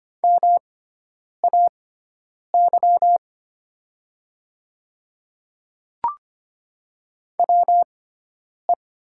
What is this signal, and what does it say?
Morse code, heard as a keyed tone, about 25 wpm, spaced as for 9 wpm, 700 Hz. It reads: MAY WE
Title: 6x Character Spacing - Top 100 Words - 25wpm